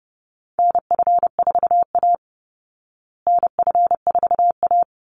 Morse code, heard as a keyed tone, about 30 wpm, 700 Hz. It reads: NF4A NF4A
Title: Call Signs Repeated 3x - 30wpm